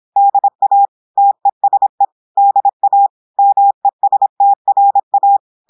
Morse code, heard as a keyed tone, about 26 wpm, 800 Hz. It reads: DA TESE DA MESTRA
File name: audios / portugais / portugais texte en clair 1 vitesse 26 mots minutes